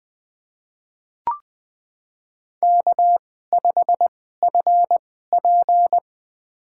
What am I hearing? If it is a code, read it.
Morse code, heard as a keyed tone, about 20 wpm, spaced as for 14 wpm, 700 Hz. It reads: K5FP
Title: USA Call Signs - 2x Character Spacing - 20wpm